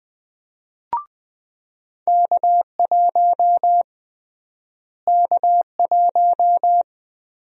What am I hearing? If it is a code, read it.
Morse code, heard as a keyed tone, about 20 wpm, 700 Hz. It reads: K1 K1
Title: USA Call Sign Prefixes - Repeated 2x - 20wpm